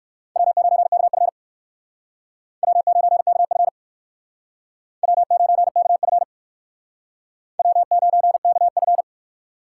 Morse code, heard as a keyed tone, about 56 wpm, 700 Hz. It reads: W9KF W9KF W9KF W9KF